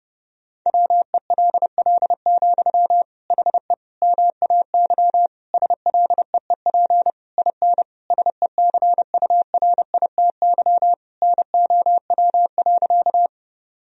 Morse code, heard as a keyed tone, about 28 wpm, 700 Hz. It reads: WELL, HE MAY SLEEP IN SECURITY NOW.
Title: Sentences from Top 1000 Words - Review Part 4 - 30wpm